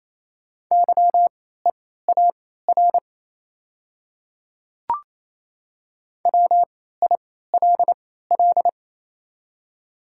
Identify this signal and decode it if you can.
Morse code, heard as a keyed tone, about 28 wpm, spaced as for 16 wpm, 700 Hz. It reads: YEAR WILL
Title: 3x Character Spacing - Top 200 Words - 28wpm